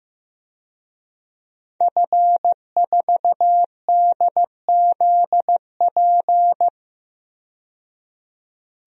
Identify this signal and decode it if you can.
Morse code, heard as a keyed tone, about 15 wpm, 700 Hz. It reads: F4DZP
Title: DXCC-100 Call Signs - 5 Characters - 15wpm